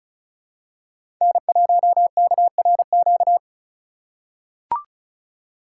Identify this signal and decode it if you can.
Morse code, heard as a keyed tone, about 36 wpm, 700 Hz. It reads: N1KRQ
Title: USA Call Signs - 5 Characters - 4x Character Spacing - 35wpm